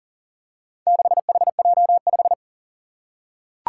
Morse code, heard as a keyed tone, about 40 wpm, 700 Hz. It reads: 6HJ5
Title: Four Challenging Letter-Number Combinations 40wpm